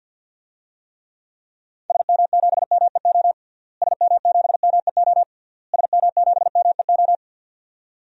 Morse code, heard as a keyed tone, about 51 wpm, 700 Hz. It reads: SM7MEO SM7MEO SM7MEO
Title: DXCC-100 Call Signs - 6 Characters - Repeated 3x - 50wpm